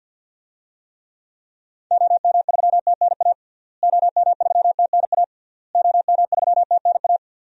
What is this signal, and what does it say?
Morse code, heard as a keyed tone, about 52 wpm, 700 Hz. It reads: OM3TNA OM3TNA OM3TNA
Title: DXCC-100 Call Signs - 6 Characters - Repeated 3x - 50wpm